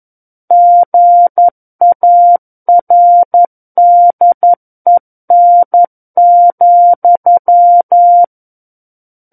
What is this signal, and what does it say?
Morse code, heard as a keyed tone, about 11 wpm, 700 Hz. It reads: GARDEN,
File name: Book One / 02-Chapter 2 - 11 WPM